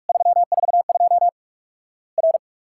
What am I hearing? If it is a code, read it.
Morse code, heard as a keyed tone, about 46 wpm, 700 Hz. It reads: 342 R
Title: Worked All Europe DX Contest - QTC Exchange - 45wpm